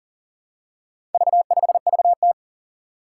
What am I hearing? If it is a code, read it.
Morse code, heard as a keyed tone, about 40 wpm, 700 Hz. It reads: V5VT
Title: Four Challenging Letter-Number Combinations 40wpm